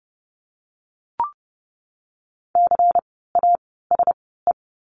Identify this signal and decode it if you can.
Morse code, heard as a keyed tone, about 30 wpm, spaced as for 17 wpm, 700 Hz. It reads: CASE
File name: Lesson 16f - Letter c - Words 30wpm